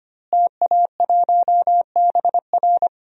Morse code, heard as a keyed tone, about 25 wpm, 700 Hz. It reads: TA1BR
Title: DXCC-100 Call Signs - 5 Characters - Repeated 2x - 25wpm